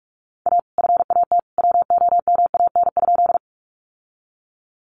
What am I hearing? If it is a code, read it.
Morse code, heard as a keyed tone, about 44 wpm, 700 Hz. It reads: A FAT WOMAN?
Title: Sentences from Top 1100 Words 45wpm